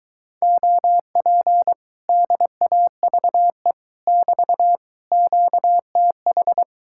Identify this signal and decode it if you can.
Morse code, heard as a keyed tone, about 23 wpm, 700 Hz. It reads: OP DAVE <BT> QTH